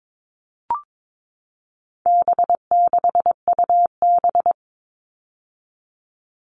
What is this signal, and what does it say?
Morse code, heard as a keyed tone, about 22 wpm, 700 Hz. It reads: B6UB
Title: All Cut Numbers - 4-Digit 22wpm